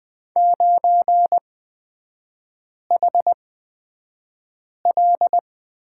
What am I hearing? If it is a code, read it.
Morse code, heard as a keyed tone, about 20 wpm, spaced as for 5 wpm, 700 Hz. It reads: 9HL
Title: 236 Lesson 38e - Letter q - Three Characters 20wpm